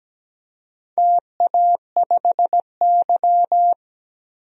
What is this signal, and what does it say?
Morse code, heard as a keyed tone, about 17 wpm, 700 Hz. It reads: TA5Y
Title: DXCC-100 Call Signs - 4 Characters - 17wpm